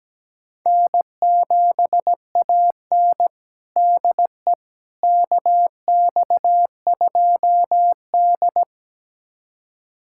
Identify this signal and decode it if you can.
Morse code, heard as a keyed tone, about 17 wpm, 700 Hz. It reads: N7AN DE KX2D
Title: USA Callsign DE Callsign - 4 Characters - 17wpm